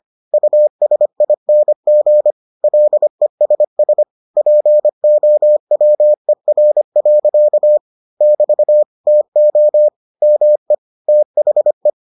Morse code, heard as a keyed tone, about 25 wpm, 600 Hz. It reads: USING LESS POWER. <BT> TO ME THE